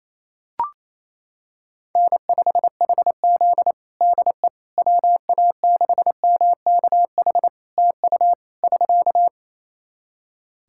Morse code, heard as a keyed tone, about 28 wpm, 700 Hz. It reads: N5HZ DE WA6MKH TU <SK>
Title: Practice QSO - Exchange 4 - Ending - 28wpm